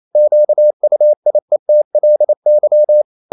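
Morse code, heard as a keyed tone, about 28 wpm, 600 Hz. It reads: QUIETLY
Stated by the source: Morse code practice at 28 wpm, text of Ken Follett's novel Never in English, from CW med Gnister.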